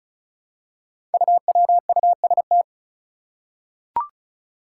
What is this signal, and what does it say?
Morse code, heard as a keyed tone, about 36 wpm, 700 Hz. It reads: UWUST
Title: Five Challenging Letter-Number Combinations 35wpm